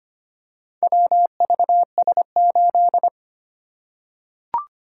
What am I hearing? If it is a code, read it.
Morse code, heard as a keyed tone, about 25 wpm, 700 Hz. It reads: WVS8
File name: practice / 25wpm / Four Letter-Number Combinations - 25wpm